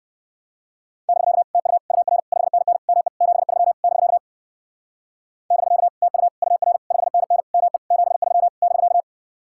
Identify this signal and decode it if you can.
Morse code, heard as a keyed tone, about 58 wpm, 700 Hz. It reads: <BK> TU UR 5NN ME 73 <BK> <BK> TU UR 5NN ME 73 <BK>